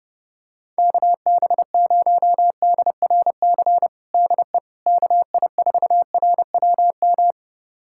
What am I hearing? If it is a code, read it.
Morse code, heard as a keyed tone, about 32 wpm, 700 Hz. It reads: KB0DRC DE KI4RWM
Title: USA Callsign DE Callsign - 6 Characters - 30wpm